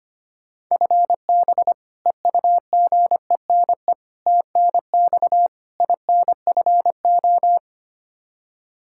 Morse code, heard as a keyed tone, about 25 wpm, 700 Hz. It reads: FB EUGENE TNX INFO
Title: Practice QSO - Exchange 3 - More Info - 25wpm